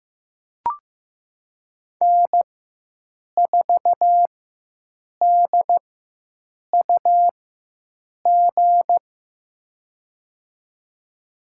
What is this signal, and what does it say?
Morse code, heard as a keyed tone, about 15 wpm, spaced as for 7 wpm, 700 Hz. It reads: N4DUG
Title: USA Call Signs - 4x Character Spacing - 15wpm